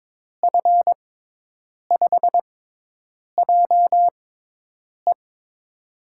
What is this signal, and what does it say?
Morse code, heard as a keyed tone, about 22 wpm, spaced as for 8 wpm, 700 Hz. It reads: F5JE
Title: USA Call Signs - 6x Character Spacing - 22wpm